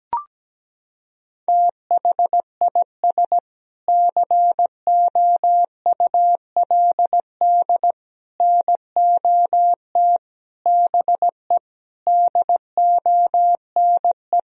Morse code, heard as a keyed tone, about 17 wpm, 700 Hz. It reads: THIS COULD NOT BE DONE
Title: Sentences from Top 1000 Words - Review Part 3 - 17wpm